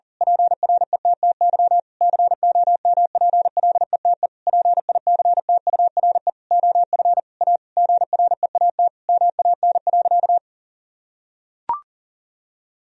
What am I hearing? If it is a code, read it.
Morse code, heard as a keyed tone, about 37 wpm, 700 Hz. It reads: PRETTY COMPLETE PICTURE OF A GREAT MAN.
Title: Sentences from Top 1000 Words - Review Part 2 - 40wpm